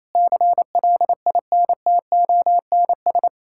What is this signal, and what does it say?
Morse code, heard as a keyed tone, about 28 wpm, 700 Hz. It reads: CLINTONS
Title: Pangram Challenge v2 - 28wpm